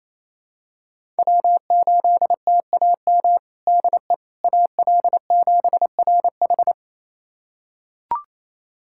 Morse code, heard as a keyed tone, about 28 wpm, 700 Hz. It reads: W8TAM DE AL7RH
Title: USA Callsign DE Callsign - 5 Characters - 28wpm